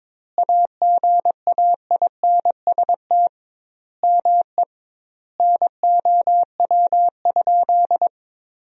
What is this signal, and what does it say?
Morse code, heard as a keyed tone, about 22 wpm, 700 Hz. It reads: AGAINST ME NOW?
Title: 2x Word-Spacing - Sentences from Top 400 Words - 22wpm